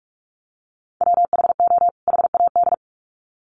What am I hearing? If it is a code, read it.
Morse code, heard as a keyed tone, about 44 wpm, 700 Hz. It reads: WHO HAD
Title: Sets of 2 Words 45wpm